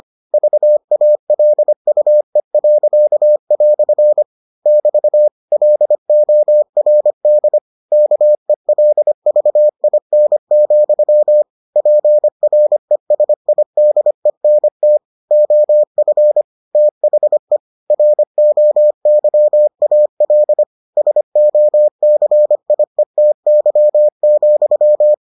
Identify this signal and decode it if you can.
Morse code, heard as a keyed tone, about 25 wpm, 600 Hz. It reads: VALUE." <BT> LORD KELVIN, PRESIDENT OF THE ROYAL SOCIETY,